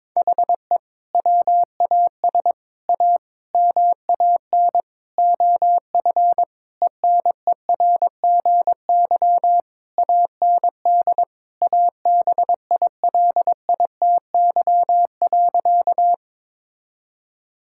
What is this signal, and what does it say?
Morse code, heard as a keyed tone, about 21 wpm, 700 Hz. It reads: HE WAS A MAN OF ENERGY AND ABILITY.
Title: Sentences from Top 1000 Words - Review Part 6 - 22wpm